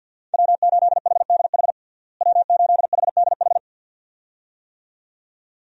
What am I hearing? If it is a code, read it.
Morse code, heard as a keyed tone, about 51 wpm, 700 Hz. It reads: W8HDH W8HDH
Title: USA Call Signs - 5 Characters - Repeated 2x - 50wpm